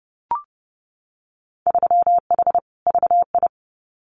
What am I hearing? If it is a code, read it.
Morse code, heard as a keyed tone, about 30 wpm, 700 Hz. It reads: 3H VI
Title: Winter Field Day Exchanges Encore 30wpm